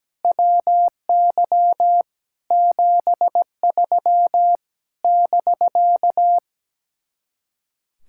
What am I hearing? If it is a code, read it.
Morse code, heard as a keyed tone, about 17 wpm, 700 Hz. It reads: WY 73 <BK>
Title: Speed-Racing Common QSO Phrases - 17wpm